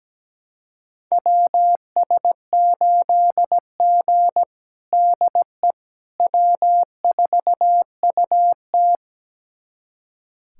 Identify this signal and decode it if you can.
Morse code, heard as a keyed tone, about 17 wpm, 700 Hz. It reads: WS8G DE W4UT
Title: USA Callsign DE Callsign - 4 Characters - 17wpm